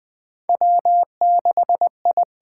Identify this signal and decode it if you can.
Morse code, heard as a keyed tone, about 20 wpm, 700 Hz. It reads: W6I
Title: USA Call Signs - 4 Characters - 8x Character Spacing - 20wpm